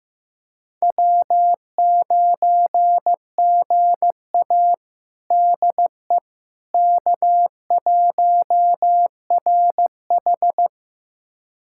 Morse code, heard as a keyed tone, about 15 wpm, 700 Hz. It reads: W9GA DE K1RH
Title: USA Callsign DE Callsign - 4 Characters - 15wpm